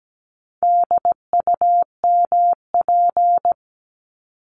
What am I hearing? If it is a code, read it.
Morse code, heard as a keyed tone, about 17 wpm, 700 Hz. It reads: DUMP
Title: Top 1001-1250 Words 17wpm